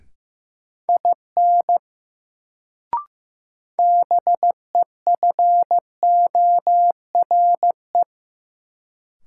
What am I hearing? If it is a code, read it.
Morse code, heard as a keyed tone, about 15 wpm, 700 Hz. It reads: IN BEFORE
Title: Sets of 2 Words - New Word Drill - 15wpm